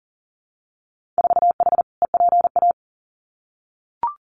Morse code, heard as a keyed tone, about 41 wpm, 700 Hz. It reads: 4H EPA
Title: Winter Field Day Exchanges 40wpm